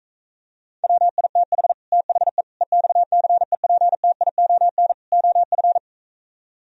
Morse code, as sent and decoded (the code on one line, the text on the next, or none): .-- .. - .... / - .... / . -..- -.-. .--. - .. --- -. / --- ..-.
WITH THE EXCEPTION OF